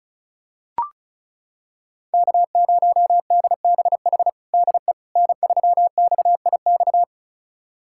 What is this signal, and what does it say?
Morse code, heard as a keyed tone, about 34 wpm, 700 Hz. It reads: K0DBH DE N3XIX
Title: USA Callsign DE Callsign - 5 Characters - 35wpm